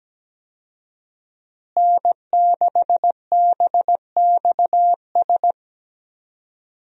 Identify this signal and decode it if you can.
Morse code, heard as a keyed tone, about 17 wpm, 700 Hz. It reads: N6BXS
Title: USA Call Signs - 5 Characters - Repeated 3x - 17wpm